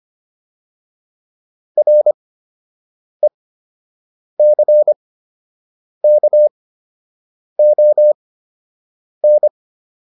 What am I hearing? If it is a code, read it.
Morse code, heard as a keyed tone, about 25 wpm, spaced as for 7 wpm, 600 Hz. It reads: RECKON